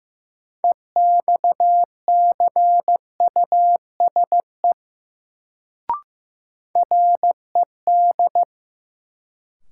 Morse code, heard as a keyed tone, about 15 wpm, 700 Hz. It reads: EXCUSE RED
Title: Sentences from Top 700 Words - New Word Drill - 15wpm